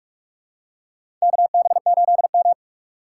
Morse code, heard as a keyed tone, about 47 wpm, 700 Hz. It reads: KB8M
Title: K1USN SST QSO Practice - 45wpm